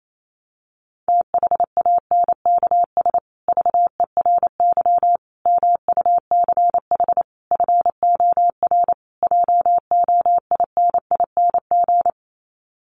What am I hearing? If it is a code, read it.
Morse code, heard as a keyed tone, about 28 wpm, 700 Hz. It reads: THANKS VERY MUCH FOR JOINING
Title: Sets of 5 Words Encore2 28wpm